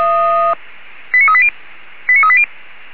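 Five-tone selective call (ZVEI), heard as quick runs of five tones, tones of about 70 ms. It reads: 89380 89380
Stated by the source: ZVEI five-tone selective call received by SDR.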